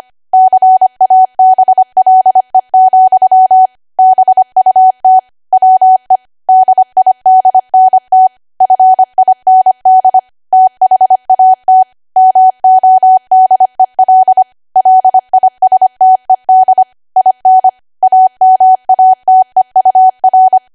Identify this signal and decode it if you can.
Morse code, heard as a keyed tone, about 25 wpm, 750 Hz. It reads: CABLE, BUT WE DIDNT FIND THAT MODEL LISTED IN AMATEUR
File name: Archive / 25 WPM / 25WPM